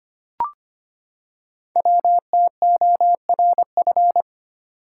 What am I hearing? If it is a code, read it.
Morse code, heard as a keyed tone, about 25 wpm, 700 Hz. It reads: WTORF
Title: Code Groups - 5 Letter:Numbers - 25wpm